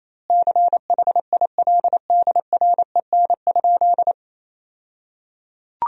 Morse code, heard as a keyed tone, about 28 wpm, 700 Hz. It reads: CHILDREN?
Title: 2x Word-Spacing - Sentences from Top 300 Words - 28wpm